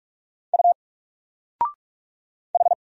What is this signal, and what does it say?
Morse code, heard as a keyed tone, about 44 wpm, 700 Hz. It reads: U H
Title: Single Letter-Number Alt 45wpm